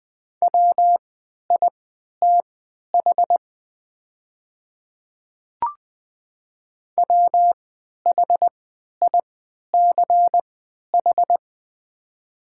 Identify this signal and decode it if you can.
Morse code, heard as a keyed tone, about 20 wpm, spaced as for 11 wpm, 700 Hz. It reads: WITH WHICH
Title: 3x Character Spacing - Top 100 Words - 20wpm